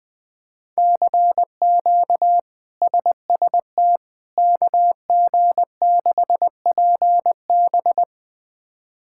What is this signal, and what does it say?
Morse code, heard as a keyed tone, about 21 wpm, 700 Hz. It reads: CQ SST KG6PB
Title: K1USN SST QSO Practice - 20wpm